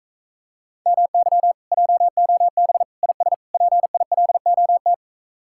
Morse code, heard as a keyed tone, about 46 wpm, 700 Hz. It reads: MY JOB IS PILOT